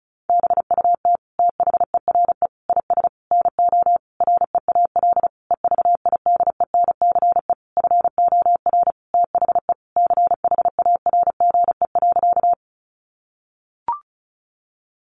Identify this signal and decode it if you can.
Morse code, heard as a keyed tone, about 34 wpm, 700 Hz. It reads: BUT THERE IS NO REAL EVIDENCE FOR THE CHARGE.
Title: Sentences from Top 1300 Words 35wpm